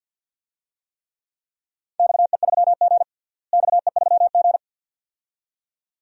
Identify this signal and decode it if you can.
Morse code, heard as a keyed tone, about 52 wpm, 700 Hz. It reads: XE3G XE3G
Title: DXCC-100 Call Signs - 4 Characters - Repeated 2x - 50wpm